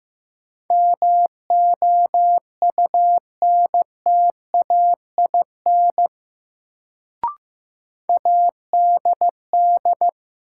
Morse code, heard as a keyed tone, about 15 wpm, 700 Hz. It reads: MOUNTAIN ADD
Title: Sentences from Top 1000 Words - Review Part 1 - New Word Drill - 15wpm